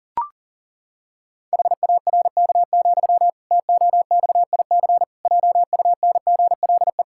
Morse code, heard as a keyed tone, about 37 wpm, 700 Hz. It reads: HARK, TOXIC JUNGLE
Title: Pangram Challenge v2 - 40wpm